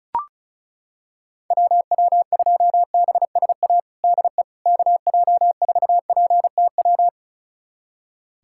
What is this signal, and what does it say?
Morse code, heard as a keyed tone, about 36 wpm, 700 Hz. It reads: WW2BSA DE KJ4PTW
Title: USA Callsign DE Callsign - 6 Characters - 35wpm